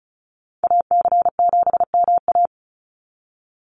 Morse code, heard as a keyed tone, about 34 wpm, 700 Hz. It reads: AC7MA
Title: NA SOTA Activator Callsigns 35wpm